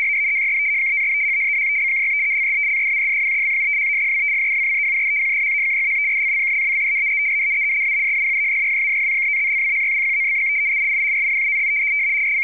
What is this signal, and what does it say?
Two-tone teleprinter signal, about 170 Hz shift.